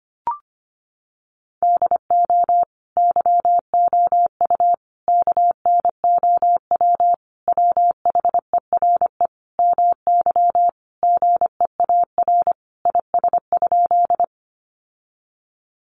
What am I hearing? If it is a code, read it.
Morse code, heard as a keyed tone, about 25 wpm, 700 Hz. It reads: DO YOU KNOW WHERE MY GEAR IS?
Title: Sentences from Top 1400 Words 25wpm